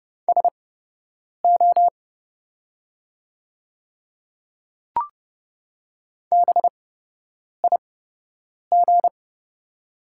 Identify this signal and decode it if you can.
Morse code, heard as a keyed tone, about 30 wpm, spaced as for 8 wpm, 700 Hz. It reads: SO BIG
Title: 8x Character Spacing - Top 200 Words - 30wpm